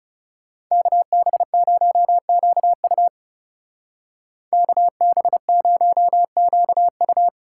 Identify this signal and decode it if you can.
Morse code, heard as a keyed tone, about 34 wpm, 700 Hz. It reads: KB0QU KB0QU
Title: Speed Racing - USA Call Signs 30wpm